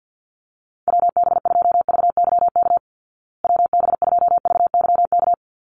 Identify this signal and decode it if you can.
Morse code, heard as a keyed tone, about 50 wpm, 700 Hz. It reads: WB2VYK WB2VYK
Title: Call Signs Repeated 2x - 50wpm